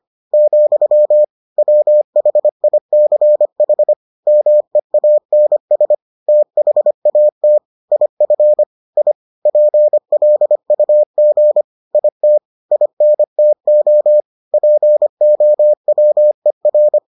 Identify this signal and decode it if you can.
Morse code, heard as a keyed tone, about 25 wpm, 600 Hz. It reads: , WHICH MEANS THAT IF I PLUG IT INTO POWER